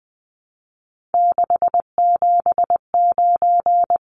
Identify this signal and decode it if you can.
Morse code, heard as a keyed tone, about 20 wpm, 700 Hz. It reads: 679
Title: Numbers - 3-Digit 20wpm